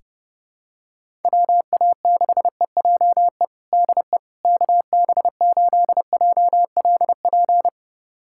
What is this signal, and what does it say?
Morse code, heard as a keyed tone, about 32 wpm, 700 Hz. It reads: WA6EJE DE KB8JLP
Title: USA Callsign DE Callsign - 6 Characters - 30wpm